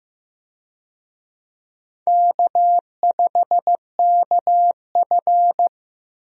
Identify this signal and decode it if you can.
Morse code, heard as a keyed tone, about 15 wpm, 700 Hz. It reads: K5KF
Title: USA Call Signs - 4 Characters - Repeated 3x - 15wpm